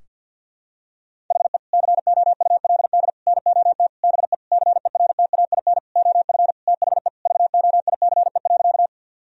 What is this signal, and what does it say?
Morse code, heard as a keyed tone, about 51 wpm, 700 Hz. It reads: HE COULD NOT BE CERTAIN OF THE VOICE.
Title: Sentences from Top 1000 Words - Review Part 2 - 50wpm